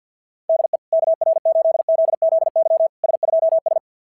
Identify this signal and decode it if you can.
Morse code, heard as a keyed tone, about 49 wpm, 690 Hz. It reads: DE KR8ZZY S2S
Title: SOTA QSO Practice 50wpm